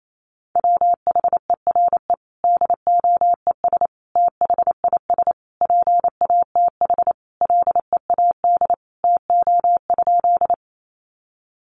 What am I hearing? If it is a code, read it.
Morse code, heard as a keyed tone, about 28 wpm, 700 Hz. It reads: WHERE DOES THIS PATH LEAD TO?
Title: Sentences from Top 1400 Words 28wpm